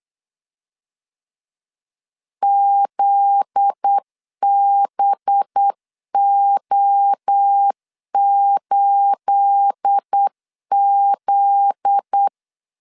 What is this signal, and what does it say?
Morse code, heard as a keyed tone, about 8 wpm, 800 Hz. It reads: ZBO8Z